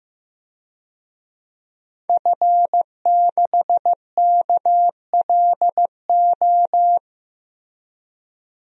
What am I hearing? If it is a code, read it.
Morse code, heard as a keyed tone, about 15 wpm, 700 Hz. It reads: F6KLO
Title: DXCC-100 Call Signs - 5 Characters - Repeated 3x - 15wpm